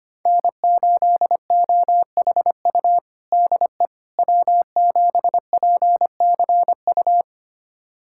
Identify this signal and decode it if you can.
Morse code, heard as a keyed tone, about 25 wpm, 700 Hz. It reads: N8OHU DE W7PCU
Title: USA Callsign DE Callsign - 5 Characters - 25wpm